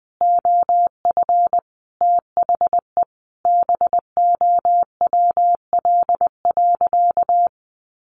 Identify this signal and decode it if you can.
Morse code, heard as a keyed tone, about 20 wpm, 700 Hz. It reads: OF THE BOWL.
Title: Sentences from Top 1200 Words 20wpm